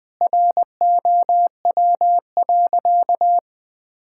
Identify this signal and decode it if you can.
Morse code, heard as a keyed tone, about 20 wpm, 700 Hz. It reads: ROW.
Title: Sentences with 3-Word Phrases - 20wpm